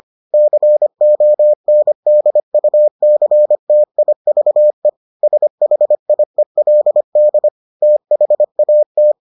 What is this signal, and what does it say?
Morse code, heard as a keyed tone, about 25 wpm, 600 Hz. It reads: CONDUCTIVE SHIELD THAT